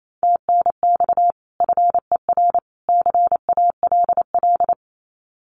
Morse code, heard as a keyed tone, about 28 wpm, 700 Hz. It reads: TNX FER CALL
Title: Common QSO Phrases 28wpm